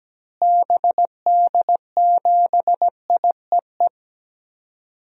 Morse code, heard as a keyed tone, about 17 wpm, 700 Hz. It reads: BD7IEE
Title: DXCC-100 Call Signs - 6 Characters - Repeated 2x - 17wpm